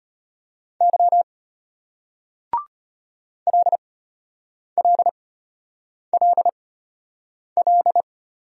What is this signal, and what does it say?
Morse code, heard as a keyed tone, about 35 wpm, spaced as for 8 wpm, 700 Hz. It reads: Y LLLL